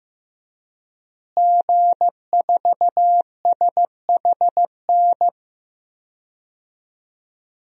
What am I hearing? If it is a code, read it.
Morse code, heard as a keyed tone, about 15 wpm, 700 Hz. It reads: G4SHN
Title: DXCC-100 Call Signs - 5 Characters - Repeated 2x - 15wpm